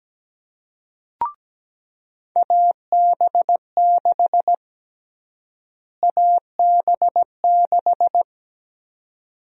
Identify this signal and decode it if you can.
Morse code, heard as a keyed tone, about 17 wpm, 700 Hz. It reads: AB6 AB6
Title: USA Call Sign Prefixes - Repeated 2x - 17wpm